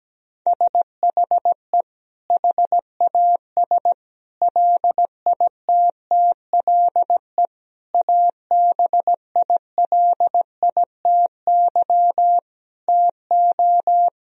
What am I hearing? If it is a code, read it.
Morse code, heard as a keyed tone, about 17 wpm, 700 Hz. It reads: SHE HAS LITTLE ABILITY TO
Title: Sentences from Top 1000 Words - Review Part 3 - 17wpm